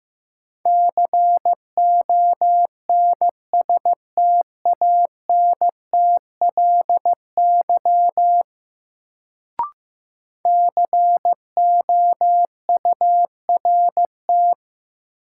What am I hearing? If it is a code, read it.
Morse code, heard as a keyed tone, about 15 wpm, 700 Hz. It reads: CONSTANTLY COURT
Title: Sentences from Top 1600 Words - New Word Drill Alt - 15wpm